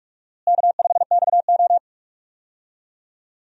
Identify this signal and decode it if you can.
Morse code, heard as a keyed tone, about 43 wpm, 700 Hz. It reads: K5XO
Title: USA Call Signs - 4 Characters - Repeated 3x - 45wpm